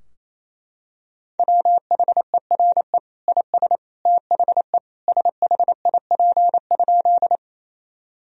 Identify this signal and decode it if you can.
Morse code, heard as a keyed tone, about 28 wpm, 700 Hz. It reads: WHERE IS THE SHIP?
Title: Sentences from Top 1000 Words - Review Part 2 - 28wpm